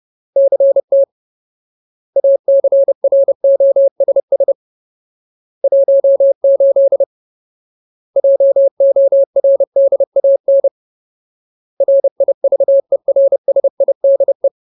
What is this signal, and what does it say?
Morse code, heard as a keyed tone, about 30 wpm, 550 Hz. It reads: CT ACROSS 18 JORDAN RIVERSIDE
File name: Ultimate Word List / 30WPM Ultimate